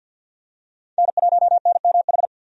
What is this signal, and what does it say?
Morse code, heard as a keyed tone, about 52 wpm, 700 Hz. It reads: N1NMH
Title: USA Call Signs - 5 Characters - 50wpm